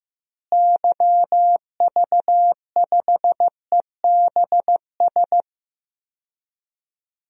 Morse code, heard as a keyed tone, about 15 wpm, 700 Hz. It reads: YV5EBS
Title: DXCC-100 Call Signs - 6 Characters - Repeated 2x - 15wpm